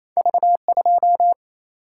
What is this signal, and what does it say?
Morse code, heard as a keyed tone, about 28 wpm, 700 Hz. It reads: V 2
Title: Single-Letter Single-Number 28wpm